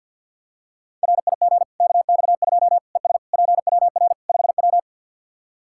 Morse code, heard as a keyed tone, about 52 wpm, 700 Hz. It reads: RIG KX2 ES PWR 5W
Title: K1USN SST QSO Practice 50wpm